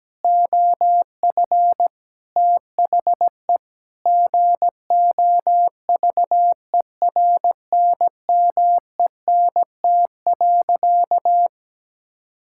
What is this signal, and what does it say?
Morse code, heard as a keyed tone, about 17 wpm, 700 Hz. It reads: OF THE GOVERNMENT.
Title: Sentences from Top 1600 Words 17wpm